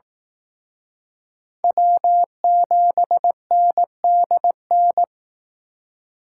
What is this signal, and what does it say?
Morse code, heard as a keyed tone, about 19 wpm, 700 Hz. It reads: W7NDN
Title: Speed Racing - USA Call Signs 15wpm